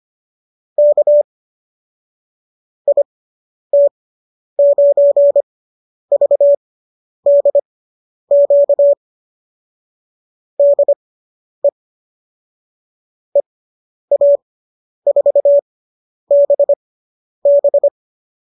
Morse code, heard as a keyed tone, about 25 wpm, spaced as for 10 wpm, 570 Hz. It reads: K IT9VDQ DE EA4BB